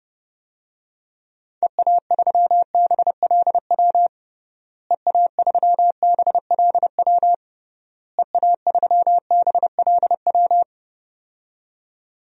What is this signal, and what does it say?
Morse code, heard as a keyed tone, about 30 wpm, 700 Hz. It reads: EA3BLW EA3BLW EA3BLW
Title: DXCC-100 Call Signs - 6 Characters - Repeated 3x - 30wpm